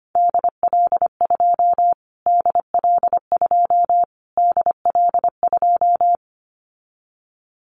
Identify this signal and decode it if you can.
Morse code, heard as a keyed tone, about 25 wpm, 700 Hz. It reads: DL2 DL2 DL2
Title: Callsign Prefixes for Top 25 DXCC - Repeated 3x - 25wpm